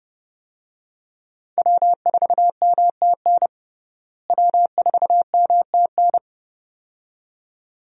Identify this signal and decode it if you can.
Morse code, heard as a keyed tone, about 30 wpm, 700 Hz. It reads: W4MTN W4MTN
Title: USA Call Signs - 5 Characters - Repeated 2x - 30wpm